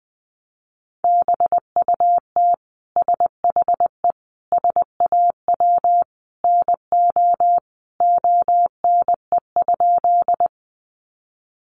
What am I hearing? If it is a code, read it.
Morse code, heard as a keyed tone, about 19 wpm, 700 Hz. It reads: BUT SHE SAW NO ONE?
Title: Sentences from Top 300 Words 20wpm